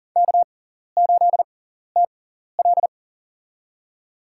Morse code, heard as a keyed tone, about 40 wpm, spaced as for 14 wpm, 700 Hz. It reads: K8TL
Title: USA Call Signs - 4 Characters - 6x Character Spacing - 40wpm